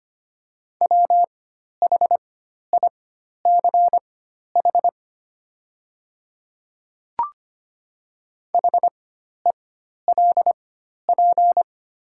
Morse code, heard as a keyed tone, about 25 wpm, spaced as for 12 wpm, 690 Hz. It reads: WHICH HELP
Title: 4x Character Spacing - Top 300 Words - 25wpm